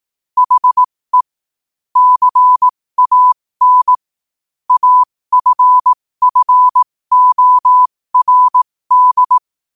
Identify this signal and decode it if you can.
Morse code, heard as a keyed tone, about 18 wpm, spaced as for 15 wpm, 1000 Hz. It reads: HE CAN AFFORD
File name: qotdfiles / 15wpm